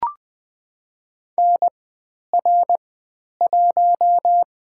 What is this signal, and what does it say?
Morse code, heard as a keyed tone, about 20 wpm, 700 Hz. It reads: N R 1
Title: 102 Lesson 17g - Letter u - Call Signs 20wpm